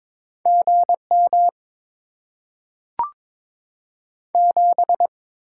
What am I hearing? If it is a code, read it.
Morse code, heard as a keyed tone, about 22 wpm, 700 Hz. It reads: GM 7
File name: Top 20 QSO Elements - 6x Character Spacing - 22wpm